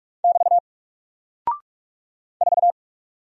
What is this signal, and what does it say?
Morse code, heard as a keyed tone, about 44 wpm, 700 Hz. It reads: <BT> 4
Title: Single Character (Letter:Number:Punctuation:Prosign) Alt 45wpm